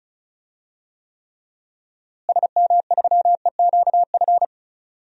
Morse code, heard as a keyed tone, about 36 wpm, 700 Hz. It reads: SM3EQF
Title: DXCC-100 Call Signs - 6 Characters - Repeated 3x - 35wpm